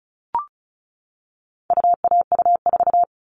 Morse code, heard as a keyed tone, about 34 wpm, 700 Hz. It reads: UAU4
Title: All Cut Numbers - 4-Digit 35wpm